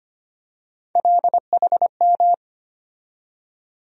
Morse code, heard as a keyed tone, about 25 wpm, 700 Hz. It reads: LHM
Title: USA Call Sign Postfixes - 3 Letters 25wpm